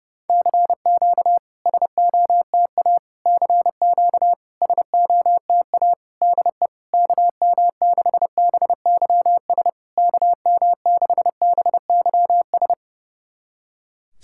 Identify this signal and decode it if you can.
Morse code, heard as a keyed tone, about 30 wpm, 700 Hz. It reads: CQ SOTA CQ SOTA DE KM6BYS KM6BYS
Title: SOTA QSO Practice 30wpm